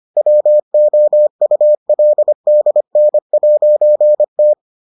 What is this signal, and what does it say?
Morse code, heard as a keyed tone, about 25 wpm, 600 Hz. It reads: WOULDN'T